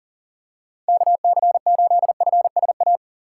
Morse code, heard as a keyed tone, about 40 wpm, 700 Hz. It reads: KC8FSA
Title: USA Call Signs - 6 Characters - 6x Character Spacing - 40wpm